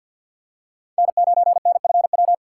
Morse code, heard as a keyed tone, about 51 wpm, 700 Hz. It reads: N9NFW
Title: USA Call Signs - 3x Character Spacing - 50wpm